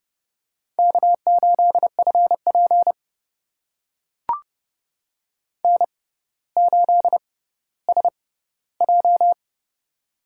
Morse code, heard as a keyed tone, about 32 wpm, spaced as for 16 wpm, 700 Hz. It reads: K8FP N8SJ
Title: USA Call Signs - 4 Characters - 6x Character Spacing - 30wpm